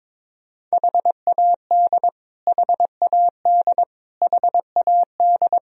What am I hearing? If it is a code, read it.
Morse code, heard as a keyed tone, about 22 wpm, 700 Hz. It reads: HAD HAD HAD
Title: Top 100 Words Repeated 22wpm